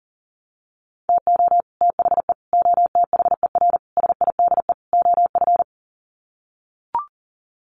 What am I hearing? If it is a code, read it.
Morse code, heard as a keyed tone, about 40 wpm, 700 Hz. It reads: TO THE OTHER SIDE OF
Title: Sets of 5 Words Encore 40wpm